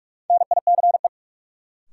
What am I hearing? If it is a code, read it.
Morse code, heard as a keyed tone, about 46 wpm, 700 Hz. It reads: NICE
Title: Sentences from Top 1100 Words - New Word Drill Alt - 45wpm